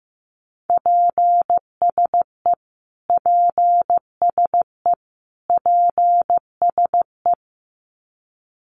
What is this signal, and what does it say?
Morse code, heard as a keyed tone, about 15 wpm, 700 Hz. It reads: PSE PSE PSE
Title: Abbreviations Repeated 3x - 15wpm